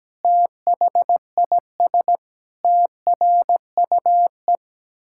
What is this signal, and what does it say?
Morse code, heard as a keyed tone, about 17 wpm, 700 Hz. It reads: THIS TRUE
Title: Pangram Challenge v2 - 17wpm